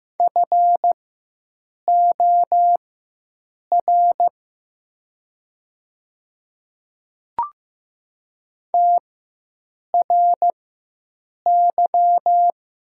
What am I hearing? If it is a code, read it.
Morse code, heard as a keyed tone, about 15 wpm, spaced as for 7 wpm, 700 Hz. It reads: FOR TRY
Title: 4x Character Spacing - Top 300 Words - 15wpm